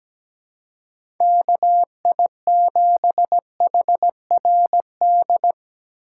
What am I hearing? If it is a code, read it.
Morse code, heard as a keyed tone, about 17 wpm, 700 Hz. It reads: KI7HRD
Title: USA Call Signs - 6 Characters - 3x Character Spacing - 17wpm